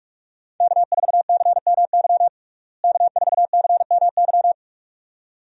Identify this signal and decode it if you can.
Morse code, heard as a keyed tone, about 47 wpm, 700 Hz. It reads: K4CMY K4CMY
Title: USA Call Signs - 5 Characters - Repeated 2x - 45wpm